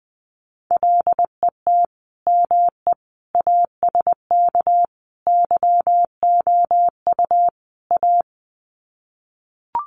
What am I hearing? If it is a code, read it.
Morse code, heard as a keyed tone, about 20 wpm, 700 Hz. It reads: LET ME ASK YOU A
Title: Sets of 5 Words Encore 20wpm